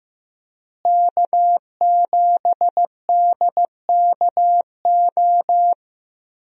Morse code, heard as a keyed tone, about 15 wpm, 700 Hz. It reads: K7DKO
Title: USA Call Signs - 5 Characters - Repeated 2x - 15wpm